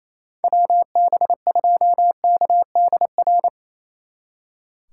Morse code, heard as a keyed tone, about 36 wpm, 700 Hz. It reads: WB2KDR